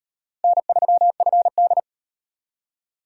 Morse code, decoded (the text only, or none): N3FD